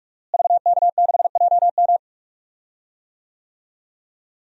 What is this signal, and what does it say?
Morse code, heard as a keyed tone, about 42 wpm, 700 Hz. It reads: VK6JM
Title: DXCC-100 Call Signs - 5 Characters - Repeated 2x - 45wpm